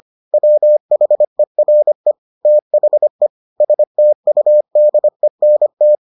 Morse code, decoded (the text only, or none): WHERE THE STUDENT